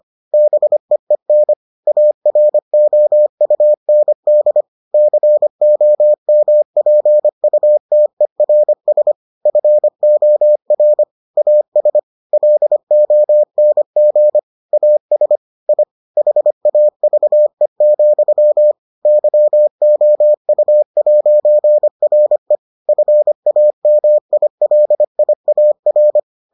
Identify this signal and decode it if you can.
Morse code, heard as a keyed tone, about 25 wpm, 600 Hz. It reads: BEEN AROUND COMPUTERS FOR AS LONG AS I HAVE, YOU'RE FAMILIAR